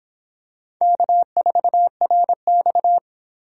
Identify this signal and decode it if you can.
Morse code, heard as a keyed tone, about 25 wpm, 700 Hz. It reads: K4RX